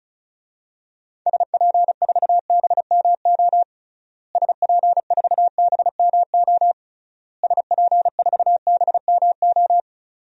Morse code, heard as a keyed tone, about 34 wpm, 700 Hz. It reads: SP4BMO SP4BMO SP4BMO
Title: DXCC-100 Call Signs - 6 Characters - Repeated 3x - 35wpm